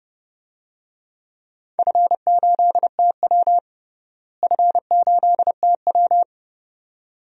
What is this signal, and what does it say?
Morse code, heard as a keyed tone, about 30 wpm, 700 Hz. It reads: F8TW F8TW
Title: DXCC-100 Call Signs - 4 Characters - Repeated 2x - 30wpm